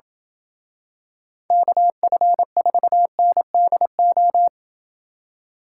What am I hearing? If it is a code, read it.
Morse code, heard as a keyed tone, about 27 wpm, 700 Hz. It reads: KF4NDO